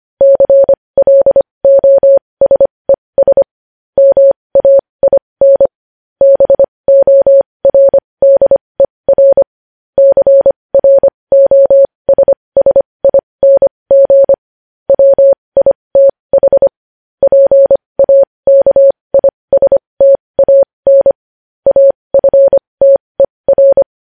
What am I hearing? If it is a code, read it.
Morse code, heard as a keyed tone, about 25 wpm, spaced as for 20 wpm, 550 Hz. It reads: CLOSES MAIN BORDER CROSSING WITH PAKISTAN AFTER
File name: podcast / News headlines-1x-25x20